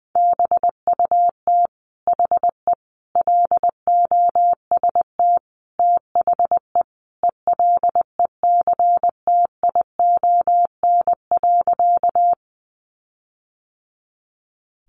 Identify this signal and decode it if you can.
Morse code, heard as a keyed tone, about 20 wpm, 700 Hz. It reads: BUT HE LOST THE ELECTION.
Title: Sentences from Top 1500 Words 20wpm